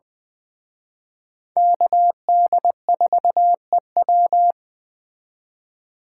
Morse code, heard as a keyed tone, about 20 wpm, 700 Hz. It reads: KD4EW
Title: Speed Racing - USA 2x2 Call Signs 17wpm